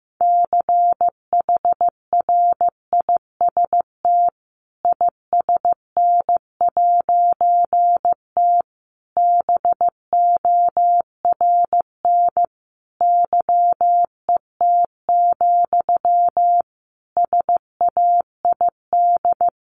Morse code, heard as a keyed tone, about 15 wpm, 700 Hz. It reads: CHRIST ISN'T BORN YET, SAID
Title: A Sound Of Thunder 15wpm